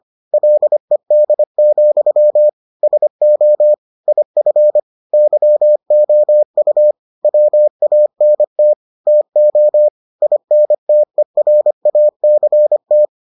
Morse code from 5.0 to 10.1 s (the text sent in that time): YOU WANT TO